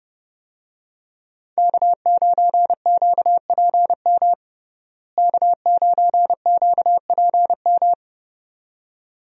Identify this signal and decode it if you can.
Morse code, heard as a keyed tone, about 30 wpm, 700 Hz. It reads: K9QPM K9QPM
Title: USA Call Signs - 5 Characters - Repeated 2x - 30wpm